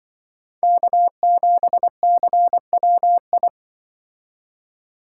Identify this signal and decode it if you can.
Morse code, heard as a keyed tone, about 24 wpm, 700 Hz. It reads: K7CWI
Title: Speed Racing - USA 1x3 Call Signs 20wpm